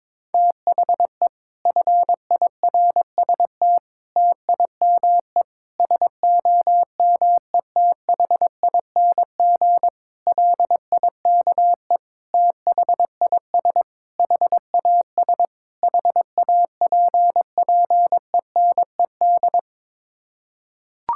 Morse code, heard as a keyed tone, about 22 wpm, 700 Hz. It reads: THE FIRST TIME SOMETHING LIKE THIS HAS HAPPENED
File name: practice / 22wpm / Sets of 8 Words - 22wpm